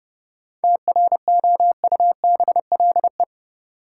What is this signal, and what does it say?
Morse code, heard as a keyed tone, about 30 wpm, 700 Hz. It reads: TROUBLE
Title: Sentences from Top 1200 Words - New Word Drill Alt - 30wpm